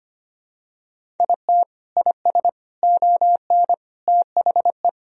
Morse code, heard as a keyed tone, about 25 wpm, 700 Hz. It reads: IT IS ON THE
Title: Sentences from Top 1000 Words - Review Part 1 - 25wpm